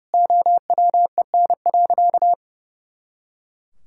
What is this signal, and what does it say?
Morse code, heard as a keyed tone, about 28 wpm, 700 Hz. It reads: OWEN.
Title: Pangram Challenge v2 - 30wpm